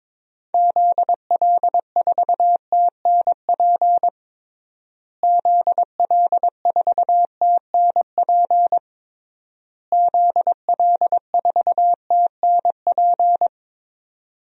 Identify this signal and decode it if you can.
Morse code, heard as a keyed tone, about 22 wpm, 700 Hz. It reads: ZL4TNP ZL4TNP ZL4TNP
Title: DXCC-100 Call Signs - 6 Characters - Repeated 3x - 22wpm